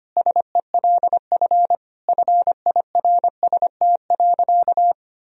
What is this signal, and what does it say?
Morse code, heard as a keyed tone, about 25 wpm, 700 Hz. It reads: SELF FIRST.
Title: Sentences from Top 1000 Words - Review Part 1 - 25wpm